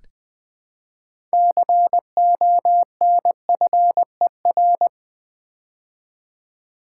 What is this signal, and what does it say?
Morse code, heard as a keyed tone, about 20 wpm, 700 Hz. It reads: CONFER
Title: Build A Word - 3 Letters At A Time - 6-Letter Words - 20wpm